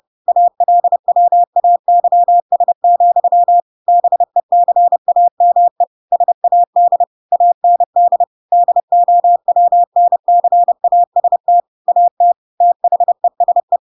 Morse code, heard as a keyed tone, about 30 wpm, 700 Hz. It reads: ALWAYS, BECAME SAD AND DOWNCAST AT THESE
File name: Pinocchio / Pinocchio ch8 30 30 ik1hge